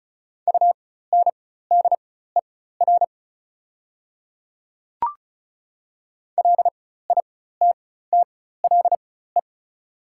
Morse code, heard as a keyed tone, about 34 wpm, spaced as for 16 wpm, 700 Hz. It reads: UNDER LITTLE
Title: 4x Character Spacing - Top 300 Words - 35wpm